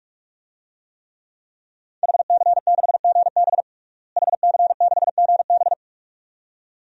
Morse code, heard as a keyed tone, about 46 wpm, 700 Hz. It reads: HC6GB HC6GB
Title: DXCC-100 Call Signs - 5 Characters - Repeated 2x - 45wpm